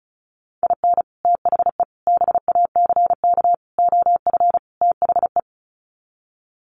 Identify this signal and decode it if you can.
Morse code, heard as a keyed tone, about 34 wpm, 700 Hz. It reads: IN THE BACK OF THE
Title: Sets of 5 Words 35wpm